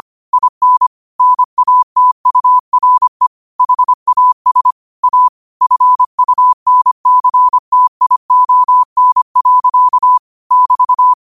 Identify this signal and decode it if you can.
Morse code, heard as a keyed tone, about 25 wpm, 1000 Hz. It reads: IN NATURE HAS A FUNCTION. <BT>